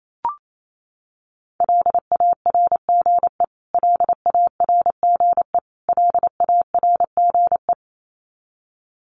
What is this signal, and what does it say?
Morse code, heard as a keyed tone, about 28 wpm, 700 Hz. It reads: LARGE LARGE LARGE
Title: Top 300 Words Repeated 28wpm